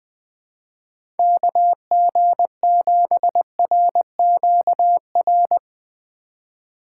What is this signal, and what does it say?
Morse code, heard as a keyed tone, about 21 wpm, 700 Hz. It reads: KG7RQR
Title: USA Call Signs - 6 Characters - 2x Character Spacing - 20wpm